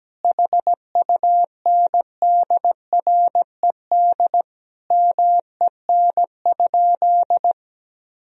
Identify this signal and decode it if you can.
Morse code, heard as a keyed tone, about 17 wpm, 700 Hz. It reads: HUNDRED MEN?
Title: Sentences from Top 1000 Words - Review Part 2 - 17wpm